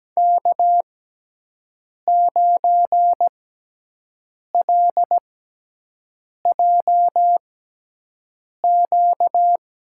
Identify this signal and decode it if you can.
Morse code, heard as a keyed tone, about 17 wpm, spaced as for 6 wpm, 700 Hz. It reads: K9LJQ
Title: USA Call Signs - 6x Character Spacing - 17wpm